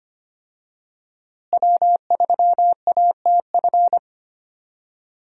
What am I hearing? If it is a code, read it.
Morse code, heard as a keyed tone, about 25 wpm, 700 Hz. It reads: W3ATF
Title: USA Call Signs - 5 Characters - Repeated 2x - 25wpm